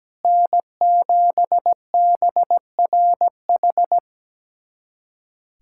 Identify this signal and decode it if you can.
Morse code, heard as a keyed tone, about 17 wpm, 700 Hz. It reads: N7BRH
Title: USA Callsign DE Callsign - 5 Characters - 17wpm